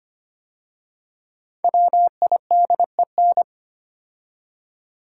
Morse code, heard as a keyed tone, about 25 wpm, 700 Hz. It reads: WIDEN
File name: 5 Letter Words - First 3 Letters Given - 25wpm